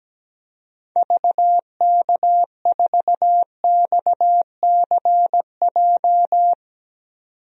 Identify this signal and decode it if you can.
Morse code, heard as a keyed tone, about 17 wpm, 700 Hz. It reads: VK4XCJ
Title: DXCC-100 Call Signs - 6 Characters - 17wpm